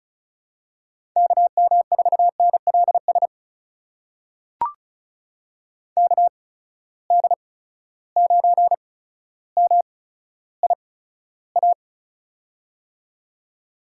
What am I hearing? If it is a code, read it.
Morse code, heard as a keyed tone, about 36 wpm, spaced as for 15 wpm, 700 Hz. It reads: KM4NLS KD9MIA
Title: USA Call Signs - 6 Characters - 8x Character Spacing - 35wpm